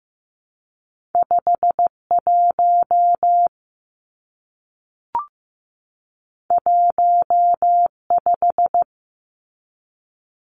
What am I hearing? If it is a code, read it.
Morse code, heard as a keyed tone, about 15 wpm, 700 Hz. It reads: 51 15
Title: Numbers - 2-Digit 15wpm